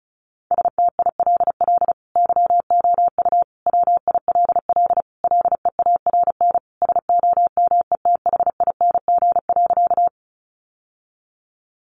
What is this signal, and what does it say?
Morse code, heard as a keyed tone, about 34 wpm, 700 Hz. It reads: STILL YOU WILL LEARN SOMETHING.
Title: Sentences from Top 600 Words 35wpm